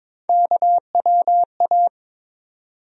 Morse code, heard as a keyed tone, about 22 wpm, 700 Hz. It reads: KWA
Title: USA Call Sign Postfixes - 3 Letters 22wpm